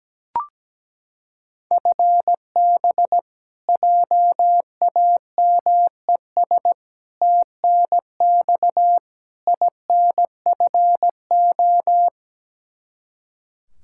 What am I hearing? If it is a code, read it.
Morse code, heard as a keyed tone, about 17 wpm, 700 Hz. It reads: FB JAMES TNX INFO
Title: Practice QSO - Exchange 3 - More Info - 17wpm